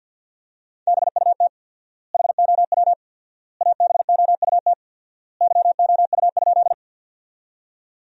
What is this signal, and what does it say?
Morse code, heard as a keyed tone, about 50 wpm, 700 Hz. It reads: BUT HOW ABOUT YOU?
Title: 4x Word-Spacing - Sentences from Top 100 Words - 50wpm